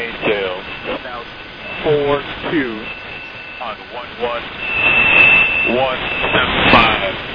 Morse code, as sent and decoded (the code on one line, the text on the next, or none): none